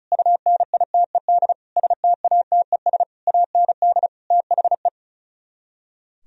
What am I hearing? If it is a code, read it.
Morse code, heard as a keyed tone, about 46 wpm, 700 Hz. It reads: UNITED STATES AND THE